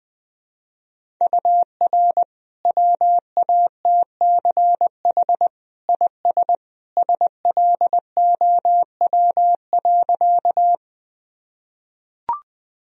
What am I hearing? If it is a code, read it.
Morse code, heard as a keyed tone, about 20 wpm, 700 Hz. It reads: UR WATCH IS SLOW.
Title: Sentences from Top 400 Words and Abbreviations 20wpm